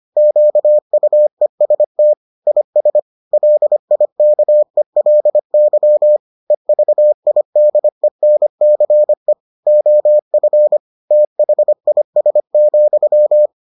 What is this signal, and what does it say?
Morse code, heard as a keyed tone, about 25 wpm, 600 Hz. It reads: QUEST IS LIKELY EVIDENCE OF THIS,